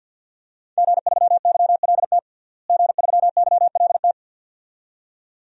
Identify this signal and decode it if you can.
Morse code, heard as a keyed tone, about 50 wpm, 700 Hz. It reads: G3YLT G3YLT
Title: DXCC-100 Call Signs - 5 Characters - Repeated 2x - 50wpm